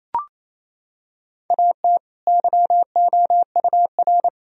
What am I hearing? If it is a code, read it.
Morse code, heard as a keyed tone, about 28 wpm, 700 Hz. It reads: AT YOUR
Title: Sentences from Top 1000 Words - Review Part 4 - 28wpm